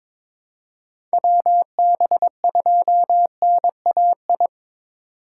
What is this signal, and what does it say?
Morse code, heard as a keyed tone, about 22 wpm, 700 Hz. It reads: WB2NAI
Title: USA Call Signs - 2x Character Spacing - 22wpm